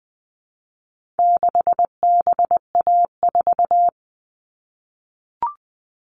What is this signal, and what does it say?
Morse code, heard as a keyed tone, about 20 wpm, 700 Hz. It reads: 6BA4
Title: All Cut Numbers - 4-Digit 20wpm